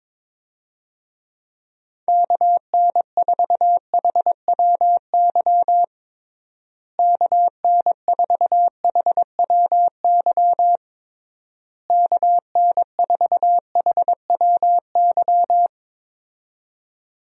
Morse code, heard as a keyed tone, about 22 wpm, 700 Hz. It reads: KN4HWY KN4HWY KN4HWY
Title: USA Call Signs - 6 Characters - Repeated 3x - 22wpm